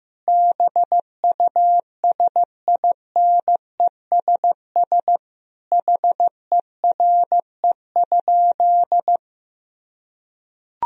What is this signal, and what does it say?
Morse code, heard as a keyed tone, about 15 wpm, 700 Hz. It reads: BUSINESS HERE?
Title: Sentences from Top 1000 Words - Review Part 2 - 15wpm